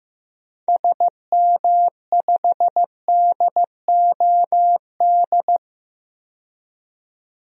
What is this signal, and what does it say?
Morse code, heard as a keyed tone, about 15 wpm, 700 Hz. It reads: SM5DOD
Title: DXCC-100 Call Signs - 6 Characters - 15wpm